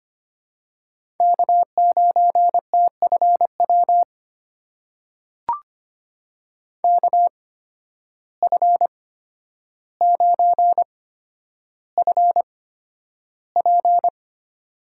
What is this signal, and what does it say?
Morse code, heard as a keyed tone, about 25 wpm, spaced as for 11 wpm, 700 Hz. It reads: K9TFW KF9FP
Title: USA Call Signs - 5 Characters - 8x Character Spacing - 25wpm